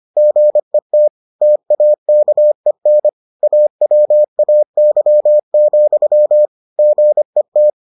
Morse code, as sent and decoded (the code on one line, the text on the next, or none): --. - / - .- -.- . -. / .- .-- .- -.-- --..-- / --. -
GET TAKEN AWAY, GET